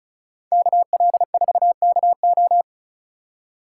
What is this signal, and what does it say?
Morse code, heard as a keyed tone, about 46 wpm, 700 Hz. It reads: KL4KO